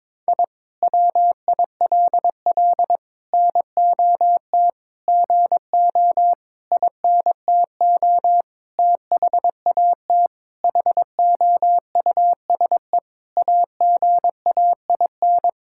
Morse code, heard as a keyed tone, about 22 wpm, 700 Hz. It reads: I WILL NOT GO INTO THAT HOUSE AGAIN
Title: Sentences from Top 1000 Words - Review Part 6 - 22wpm